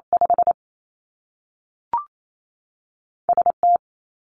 Morse code, heard as a keyed tone, about 28 wpm, 700 Hz. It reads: 5 S T
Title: Two Challenging Letter-Number Combinations 28wpm